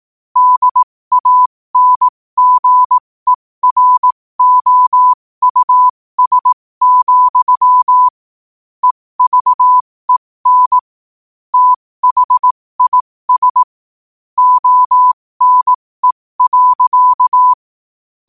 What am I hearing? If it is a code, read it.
Morse code, heard as a keyed tone, about 18 wpm, spaced as for 15 wpm, 1000 Hz. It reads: DANGEROUS, EVEN THIS ONE.